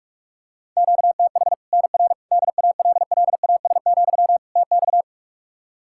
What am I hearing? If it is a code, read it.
Morse code, heard as a keyed tone, about 46 wpm, 700 Hz. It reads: QTH NR DALLAS, TX
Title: Practice QSO - Exchange 1 - Beginning - 45wpm